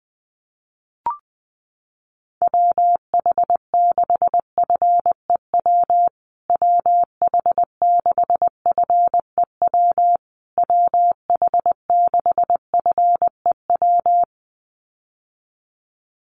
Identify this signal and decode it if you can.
Morse code, heard as a keyed tone, about 20 wpm, 700 Hz. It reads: WH6FEW WH6FEW WH6FEW
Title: USA Call Signs Repeated 3x 20wpm